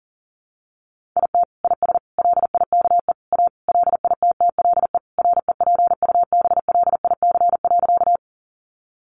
Morse code, heard as a keyed tone, about 40 wpm, 700 Hz. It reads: IT IS LIKE A LITTLE REPUBLIC.
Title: Sentences from Top 1500 Words 40wpm